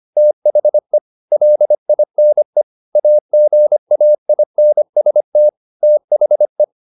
Morse code, heard as a keyed tone, about 25 wpm, 600 Hz. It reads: THE LINE AGAINST THE
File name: the old man and the sea part 1 25wpm